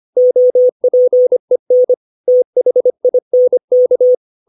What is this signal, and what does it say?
Morse code, heard as a keyed tone, about 25 wpm, 500 Hz. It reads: OPEN THINK
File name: 632 words / 632 Words 25WPM